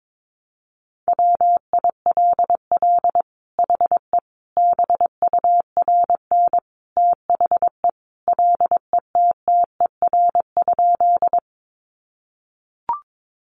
Morse code, heard as a keyed tone, about 22 wpm, 700 Hz. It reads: WILL HE BURN THE LETTER?
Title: Sentences from Top 1300 Words 22wpm